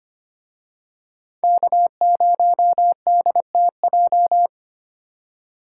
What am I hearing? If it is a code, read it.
Morse code, heard as a keyed tone, about 29 wpm, 700 Hz. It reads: K0DTJ
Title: Speed Racing - USA 1x3 Call Signs 25wpm